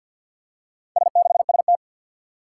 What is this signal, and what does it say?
Morse code, heard as a keyed tone, about 52 wpm, 700 Hz. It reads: S6ST
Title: Four Challenging Letter-Number Combinations 50wpm